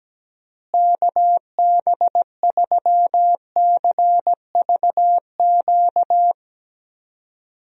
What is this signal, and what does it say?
Morse code, heard as a keyed tone, about 17 wpm, 700 Hz. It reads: KB3CVQ
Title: USA Call Signs - 6 Characters - Repeated 3x - 17wpm